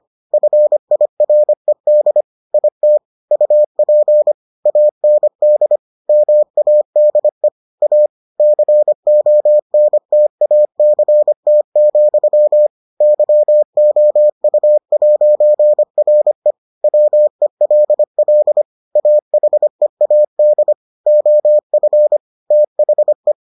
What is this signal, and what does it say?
Morse code, heard as a keyed tone, about 25 wpm, 600 Hz. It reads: FIRED IT UP AND MADE A CONTACT, YOU'RE WELL AHEAD OF THE